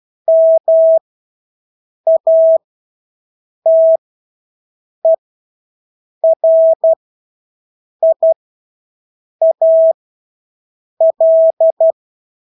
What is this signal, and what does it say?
Morse code, heard as a keyed tone, about 12 wpm, spaced as for 6 wpm, 650 Hz. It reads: MATERIAL
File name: Engelske ord 12-6 wpm